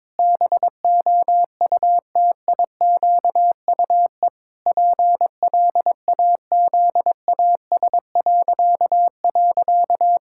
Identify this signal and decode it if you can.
Morse code, heard as a keyed tone, about 22 wpm, 700 Hz. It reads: BOUTIQUE PLAZAS..
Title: Pangram Challenge v2 - 22wpm